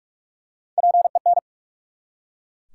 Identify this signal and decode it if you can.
Morse code, heard as a keyed tone, about 44 wpm, 700 Hz. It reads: PEN
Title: Three Letter Words - 45wpm